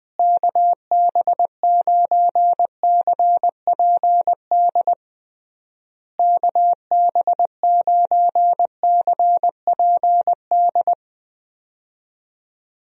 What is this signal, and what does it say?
Morse code, heard as a keyed tone, about 20 wpm, 700 Hz. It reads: KB9CPD KB9CPD
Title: USA Call Signs - 6 Characters - Repeated 3x - 20wpm